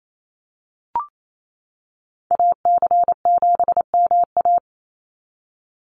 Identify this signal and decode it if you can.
Morse code, heard as a keyed tone, about 28 wpm, 700 Hz. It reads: AC7MA
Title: NA SOTA Activator Callsigns 28wpm